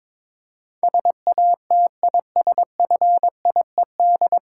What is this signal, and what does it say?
Morse code, heard as a keyed tone, about 22 wpm, 700 Hz. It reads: SATISFIED
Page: Morse Code Ninja practice 22wpm